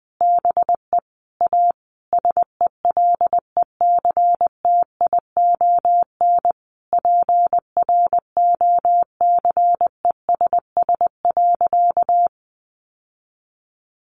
Morse code, heard as a keyed tone, about 20 wpm, 700 Hz. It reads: BE A SELECTION PROCESS.
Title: Sentences from Top 1400 Words 20wpm